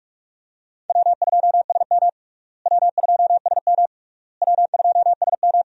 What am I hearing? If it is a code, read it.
Morse code, heard as a keyed tone, about 46 wpm, 700 Hz. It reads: W2SM W2SM W2SM
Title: USA Call Signs - 4 Characters - Repeated 3x - 45wpm